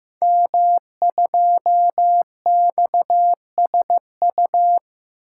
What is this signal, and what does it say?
Morse code, heard as a keyed tone, about 15 wpm, 700 Hz. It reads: M2XSU
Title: Code Groups - 5 Letter:Numbers - 15wpm